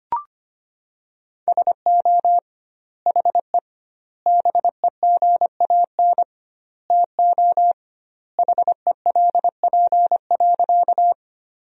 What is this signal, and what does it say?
Morse code, heard as a keyed tone, about 25 wpm, 700 Hz. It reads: SO HE BEGAN TO HELP.
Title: 2x Word-Spacing - Sentences from Top 300 Words - 25wpm